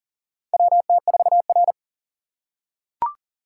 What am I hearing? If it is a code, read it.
Morse code, heard as a keyed tone, about 40 wpm, 700 Hz. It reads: WT4R
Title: USA Call Signs - 4 Characters - 2x Character Spacing - 40wpm